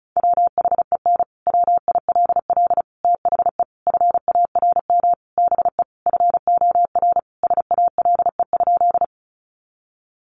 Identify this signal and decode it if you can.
Morse code, heard as a keyed tone, about 34 wpm, 700 Hz. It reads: WHEN WILL THE FARM BE FOR SALE?
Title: Sentences from Top 1400 Words 35wpm